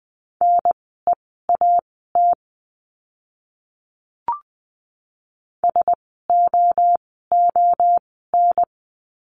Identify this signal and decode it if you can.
Morse code, heard as a keyed tone, about 20 wpm, spaced as for 14 wpm, 700 Hz. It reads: NEAT SOON
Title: 054 Lesson 10f - Letter r - Words 20wpm